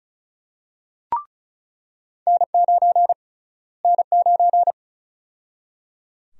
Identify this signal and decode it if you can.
Morse code, heard as a keyed tone, about 34 wpm, 700 Hz. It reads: N9 N9
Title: USA Call Sign Prefixes - Repeated 2x - 35wpm